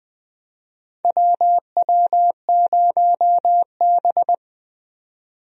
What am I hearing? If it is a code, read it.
Morse code, heard as a keyed tone, about 20 wpm, 700 Hz. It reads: WW0B
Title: USA Call Signs - 4 Characters - 20wpm